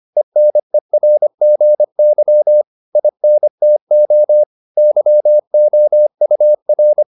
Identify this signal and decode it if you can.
Morse code, heard as a keyed tone, about 25 wpm, 600 Hz. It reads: ENERGY INTO YOUR